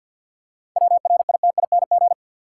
Morse code, heard as a keyed tone, about 50 wpm, 700 Hz. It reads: WRITING
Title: Seven Letter Words - 50wpm